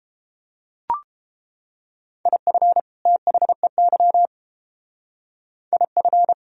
Morse code, heard as a keyed tone, about 30 wpm, 700 Hz. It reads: IF THEY IF